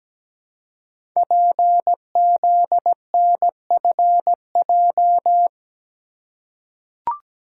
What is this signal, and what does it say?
Morse code, heard as a keyed tone, about 17 wpm, 700 Hz. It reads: PZNFJ
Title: Code Groups - 5 Letters - 17wpm